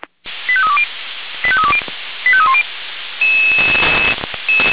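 Five-tone selective call (ZVEI), heard as quick runs of five tones, tones of about 70 ms. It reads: 85320 85320 85310
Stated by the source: ZVEI five-tone selective call received with an SDR.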